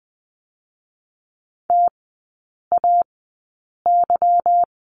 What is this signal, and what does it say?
Morse code, heard as a keyed tone, about 20 wpm, 700 Hz. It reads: T A Y
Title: 145 Lesson 24e - Letter y - Three Characters 20wpm